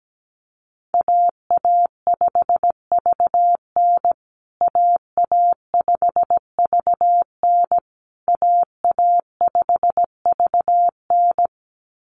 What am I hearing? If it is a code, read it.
Morse code, heard as a keyed tone, about 17 wpm, 700 Hz. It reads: AA5VN AA5VN AA5VN
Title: USA Call Signs Repeated 3x 17wpm